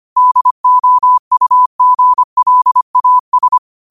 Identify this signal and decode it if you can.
Morse code, heard as a keyed tone, about 25 wpm, 1000 Hz. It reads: DOUGLAS